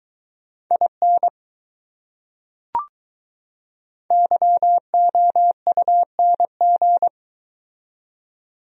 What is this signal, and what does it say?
Morse code, heard as a keyed tone, about 21 wpm, 700 Hz. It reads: IN YOUNG